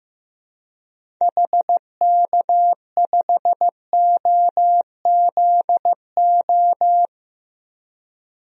Morse code, heard as a keyed tone, about 15 wpm, 700 Hz. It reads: HK5OZO
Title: DXCC-100 Call Signs - 6 Characters - 15wpm